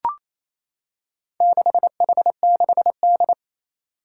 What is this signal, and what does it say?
Morse code, heard as a keyed tone, about 29 wpm, 700 Hz. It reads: 6H6D
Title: Four Challenging Letter-Number Combinations 28wpm